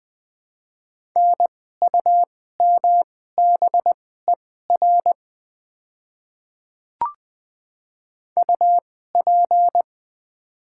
Morse code, heard as a keyed tone, about 20 wpm, spaced as for 14 wpm, 700 Hz. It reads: NUMBER UP
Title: 2x Character Spacing - Top 100 Words - 20wpm